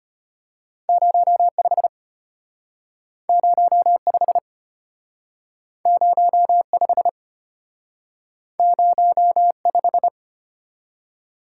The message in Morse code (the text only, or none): 05 05 05 05